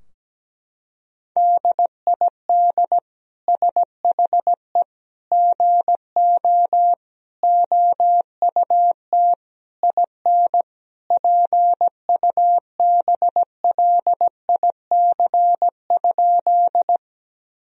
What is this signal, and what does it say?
Morse code, heard as a keyed tone, about 17 wpm, 700 Hz. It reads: DID SHE GO OUT IN PUBLIC?
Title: Sentences from Top 1000 Words - Review Part 3 - 17wpm